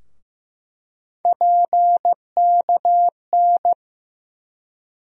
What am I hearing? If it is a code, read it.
Morse code, heard as a keyed tone, about 15 wpm, 700 Hz. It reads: PKN
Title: USA Call Sign Postfixes - 3 Letters 15wpm